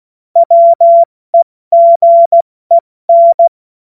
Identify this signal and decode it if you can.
Morse code, heard as a keyed tone, about 16 wpm, 680 Hz. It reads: WEGEN
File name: Morsekurs / Witze / Witz 01. März 2023 70 F80